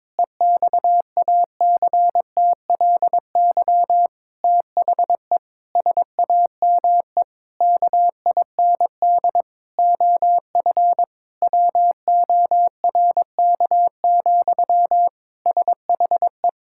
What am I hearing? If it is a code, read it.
Morse code, heard as a keyed tone, about 22 wpm, 700 Hz. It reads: EXACTLY THE SAME KIND OF WORK, SHE
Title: Sentences from Top 1000 Words - Review Part 6 - 22wpm